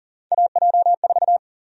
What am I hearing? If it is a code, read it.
Morse code, heard as a keyed tone, about 40 wpm, 700 Hz. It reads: AJ4
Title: USA Call Sign Prefixes - 40wpm